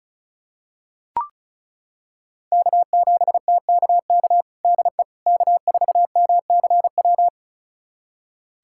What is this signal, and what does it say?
Morse code, heard as a keyed tone, about 36 wpm, 700 Hz. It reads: K7TKK DE K4MCW
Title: USA Callsign DE Callsign - 5 Characters - 35wpm